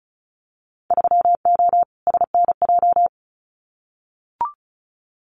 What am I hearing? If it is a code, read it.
Morse code, heard as a keyed tone, about 34 wpm, 700 Hz. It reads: 3O SNJ
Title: Winter Field Day Exchanges Encore 35wpm